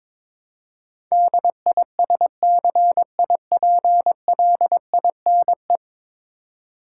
Morse code, heard as a keyed tone, about 22 wpm, 700 Hz. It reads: DISCIPLINE
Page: morse-code-ninja-practice-22wpm